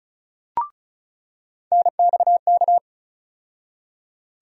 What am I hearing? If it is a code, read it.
Morse code, heard as a keyed tone, about 34 wpm, 700 Hz. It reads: NXK
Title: USA Call Sign Postfixes - 3 Letters 35wpm